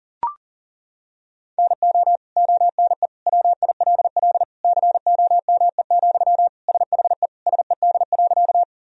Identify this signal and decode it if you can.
Morse code, heard as a keyed tone, about 40 wpm, 700 Hz. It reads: NO ONE WILL COME, SHE SED.
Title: Sentences from Top 100 Words and Abbreviations 40wpm